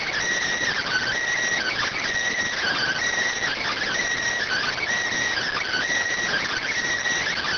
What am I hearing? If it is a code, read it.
Emergency Alert System (SAME) tones (header bursts), heard as harsh buzzing bursts. It is heard from 0.8 to 6.7 s.